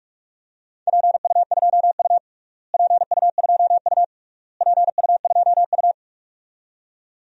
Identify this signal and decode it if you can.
Morse code, heard as a keyed tone, about 46 wpm, 700 Hz. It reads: PU2U PU2U PU2U
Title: DXCC-100 Call Signs - 4 Characters - Repeated 3x - 45wpm